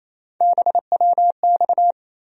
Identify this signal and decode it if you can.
Morse code, heard as a keyed tone, about 28 wpm, 700 Hz. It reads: BWX
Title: USA Call Sign Postfixes - 3 Letters 28wpm